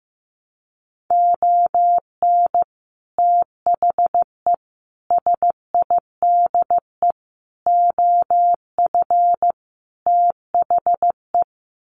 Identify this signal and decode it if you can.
Morse code, heard as a keyed tone, about 15 wpm, 700 Hz. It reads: ON THE SIDE OF THE
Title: Sets of 5 Words 15wpm